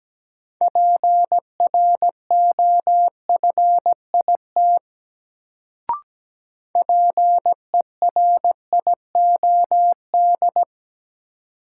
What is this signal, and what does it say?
Morse code, heard as a keyed tone, about 17 wpm, 700 Hz. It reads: PROFIT PERIOD
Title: Sentences from Top 800 Words - New Word Drill - 17wpm